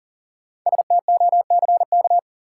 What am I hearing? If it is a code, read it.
Morse code, heard as a keyed tone, about 39 wpm, 700 Hz. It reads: STOCK
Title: Sentences from Top 800 Words - New Word Drill - 40wpm